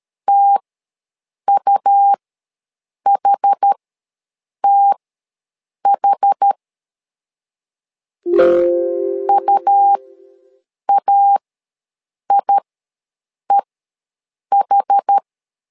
Morse code, heard as a keyed tone, about 13 wpm, spaced as for 7 wpm, 800 Hz. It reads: TUHTH UAIEH
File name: LEZIONI / 2 UHDN / 2 UH ws 12wpm 1730